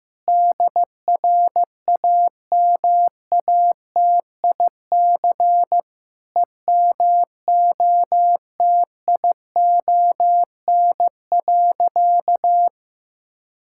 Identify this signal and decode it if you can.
Morse code, heard as a keyed tone, about 15 wpm, 700 Hz. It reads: DRAMATIC EMOTION.
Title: Sentences from Top 1600 Words 15wpm